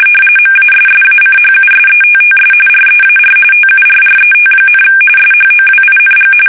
RTTY, about 45.45 baud. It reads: MASSACHUSETTS REAR TURRET, SHOWING
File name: fsk850 / USS-Massachusetts